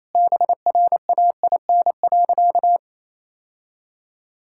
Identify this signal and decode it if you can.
Morse code, heard as a keyed tone, about 28 wpm, 700 Hz. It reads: BRAIN.
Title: Sentences from Top 1000 Words - Review Part 6 - 28wpm